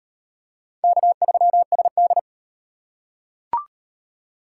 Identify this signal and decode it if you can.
Morse code, heard as a keyed tone, about 37 wpm, 700 Hz. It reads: K3SD